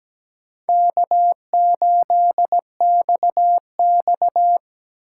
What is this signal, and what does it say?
Morse code, heard as a keyed tone, about 17 wpm, 700 Hz. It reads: K8XX
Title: USA Call Signs - 3x Character Spacing - 17wpm